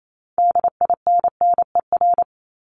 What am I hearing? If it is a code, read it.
Morse code, heard as a keyed tone, about 28 wpm, 700 Hz. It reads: DINNER
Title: Top 1751-2000 Words 28wpm